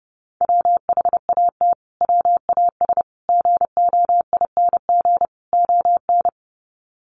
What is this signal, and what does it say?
Morse code, heard as a keyed tone, about 30 wpm, 700 Hz. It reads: WHAT WAS GOING ON
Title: Sets of 4 Words 30wpm